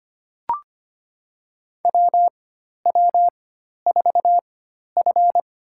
Morse code, heard as a keyed tone, about 25 wpm, spaced as for 12 wpm, 700 Hz. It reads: WW4F
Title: USA Call Signs - 4x Character Spacing - 25wpm